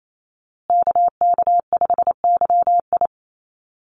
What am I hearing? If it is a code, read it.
Morse code, heard as a keyed tone, about 28 wpm, 700 Hz. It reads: KK5YI
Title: USA Call Signs Repeated 2x 28wpm